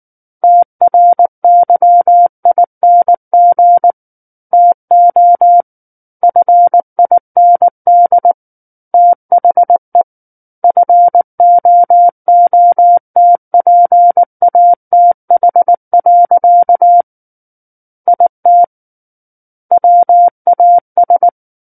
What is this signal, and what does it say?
Morse code, heard as a keyed tone, about 19 wpm, 700 Hz. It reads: TRYING TO FIND THE FOOTPATH. IT WAS
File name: Book One / 10-Chapter 10 - 19 WPM